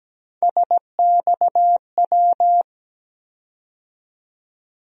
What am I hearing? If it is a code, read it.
Morse code, heard as a keyed tone, about 17 wpm, 700 Hz. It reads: SXW
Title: USA Call Sign Postfixes - 3 Letters 17wpm